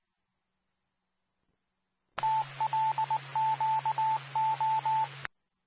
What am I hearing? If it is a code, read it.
Morse code, heard as a keyed tone, about 19 wpm, 880 Hz, with radio noise. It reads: TLQO
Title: Nassau and Suffolk County Amateur Repeaters